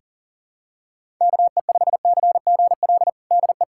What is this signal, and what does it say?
Morse code, heard as a keyed tone, about 37 wpm, 700 Hz. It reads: KE5CGL DE
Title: USA Callsign DE Callsign - 6 Characters - 40wpm